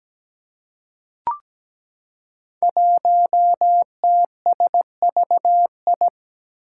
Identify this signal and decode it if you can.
Morse code, heard as a keyed tone, about 17 wpm, 700 Hz. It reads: 1TSVI
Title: Five Challenging Letter-Number Combinations 17wpm